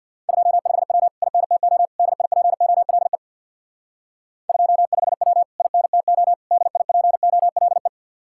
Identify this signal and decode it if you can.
Morse code, heard as a keyed tone, about 62 wpm, 700 Hz. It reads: 25W INTO DIPOLE 25W INTO DIPOLE
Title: Speed-Racing Common QSO Phrases - 50wpm